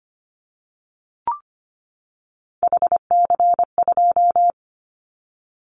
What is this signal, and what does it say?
Morse code, heard as a keyed tone, about 25 wpm, 700 Hz. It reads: HC2
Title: Callsign Prefixes for Top 100 DXCC - 25wpm